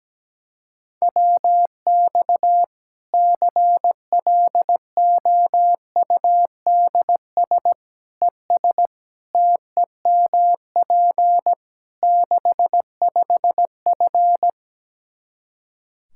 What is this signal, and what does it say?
Morse code, heard as a keyed tone, about 17 wpm, 700 Hz. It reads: WX CLOUDS ES TEMP 65F
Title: Practice QSO - Exchange 2 - Info - 17wpm